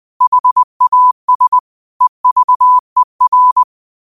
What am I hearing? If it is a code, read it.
Morse code, heard as a keyed tone, about 20 wpm, 1000 Hz. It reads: HAS EVER
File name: qotdfiles / 20wpm